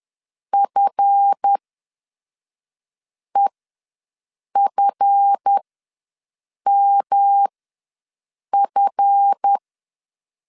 Morse code, heard as a keyed tone, about 10 wpm, spaced as for 5 wpm, 800 Hz. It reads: FEFMF
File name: LEZIONI / 7 GF9Z / 7 GF ws 10wpm 3918